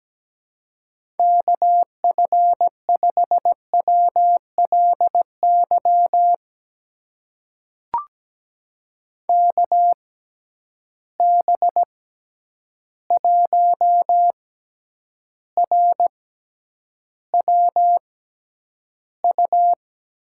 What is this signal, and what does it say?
Morse code, heard as a keyed tone, about 17 wpm, spaced as for 9 wpm, 700 Hz. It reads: KF5WLY KB1RWU
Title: USA Call Signs - 6 Characters - 6x Character Spacing - 17wpm